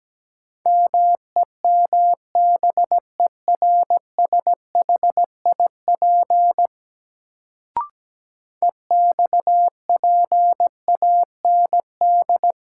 Morse code, heard as a keyed tone, about 17 wpm, 700 Hz. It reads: MEMBERSHIP EXPAND
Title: Sentences from Top 1400 Words - New Word Drill Alt - 17wpm